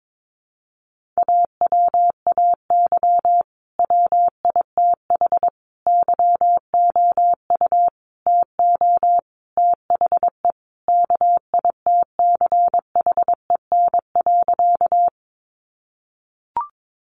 Morse code, heard as a keyed tone, about 22 wpm, 700 Hz. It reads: AWAY WITH YOU TO THE KITCHEN.
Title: Sentences from Top 1400 Words 22wpm